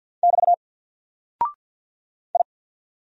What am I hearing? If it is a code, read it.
Morse code, heard as a keyed tone, about 50 wpm, 700 Hz. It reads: <BT> I
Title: Single Character (Letter:Number:Punctuation:Prosign) 50wpm